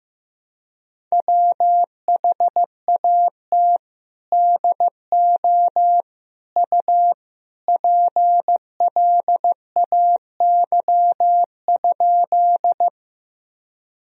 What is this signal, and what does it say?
Morse code, heard as a keyed tone, about 15 wpm, 700 Hz. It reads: WHAT DO U PLAY?
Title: Sentences from Top 200 Words and Abbreviations 15wpm